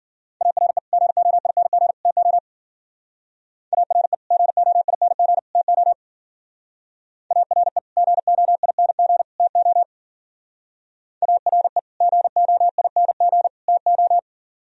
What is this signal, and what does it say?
Morse code, heard as a keyed tone, about 49 wpm, 700 Hz. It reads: ARE GOING TO ARE GOING TO ARE GOING TO ARE GOING TO
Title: Speed-Racing - Sets of 3 Words 40wpm